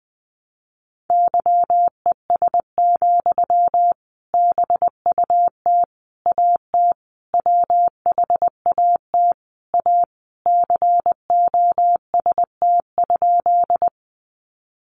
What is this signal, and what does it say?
Morse code, heard as a keyed tone, about 20 wpm, 700 Hz. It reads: YES, BUT AT WHAT A COST?
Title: Sentences from Top 800 Words 20wpm